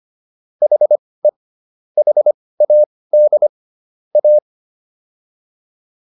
Morse code, heard as a keyed tone, about 25 wpm, spaced as for 18 wpm, 600 Hz. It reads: HE HAD A